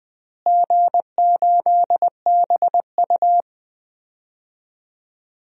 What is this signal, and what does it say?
Morse code, heard as a keyed tone, about 20 wpm, 700 Hz. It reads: G8BU
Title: DXCC-100 Call Signs - 4 Characters - Repeated 3x - 20wpm